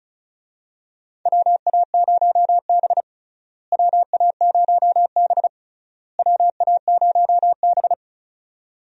Morse code, heard as a keyed tone, about 34 wpm, 700 Hz. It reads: WA0B WA0B WA0B
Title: USA Call Signs - 4 Characters - Repeated 3x - 35wpm